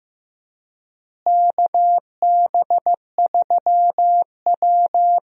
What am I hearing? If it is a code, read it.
Morse code, heard as a keyed tone, about 15 wpm, 700 Hz. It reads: KB3W
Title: USA Call Signs - 4 Characters - Repeated 2x - 15wpm